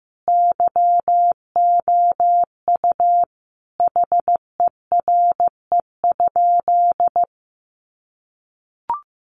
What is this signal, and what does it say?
Morse code, heard as a keyed tone, about 15 wpm, 700 Hz. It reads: YOU HERE?
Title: Sentences from Top 200 Words 15wpm